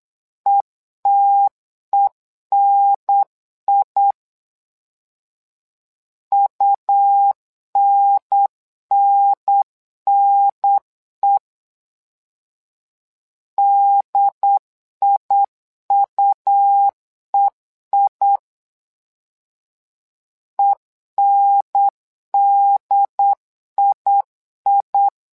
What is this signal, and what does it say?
Morse code, heard as a keyed tone, about 8 wpm, 800 Hz. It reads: ETENI UNNNE DIUEI ENDII
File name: LEZIONI / 2 UHDN / 2 UHDN 8wpm2232